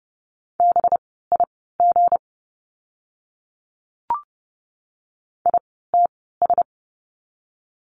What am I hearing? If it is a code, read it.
Morse code, heard as a keyed tone, about 30 wpm, spaced as for 17 wpm, 700 Hz. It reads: BIG ITS
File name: course / 30:10wpm / 237 Lesson 38f - Letter q - Words 30wpm